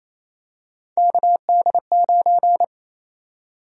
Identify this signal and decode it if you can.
Morse code, heard as a keyed tone, about 27 wpm, 700 Hz. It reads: KD9
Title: USA Call Sign Prefixes - Repeated 2x - 28wpm